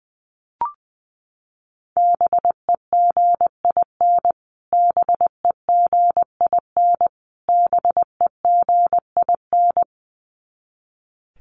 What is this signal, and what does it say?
Morse code, heard as a keyed tone, about 20 wpm, 700 Hz. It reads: BEGIN BEGIN BEGIN
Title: Top 500 Words Repeated 20wpm